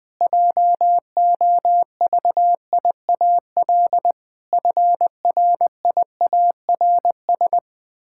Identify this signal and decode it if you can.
Morse code, heard as a keyed tone, about 20 wpm, 700 Hz. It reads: JOVIAL FRIARS
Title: Pangram Challenge v2 - 20wpm